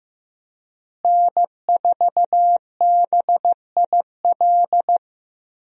Morse code, heard as a keyed tone, about 15 wpm, 700 Hz. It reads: N4BIL
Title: USA Call Signs - 5 Characters - Repeated 2x - 15wpm